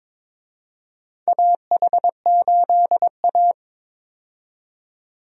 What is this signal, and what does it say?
Morse code, heard as a keyed tone, about 22 wpm, 700 Hz. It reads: AH8A
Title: USA Call Signs - 4 Characters - 22wpm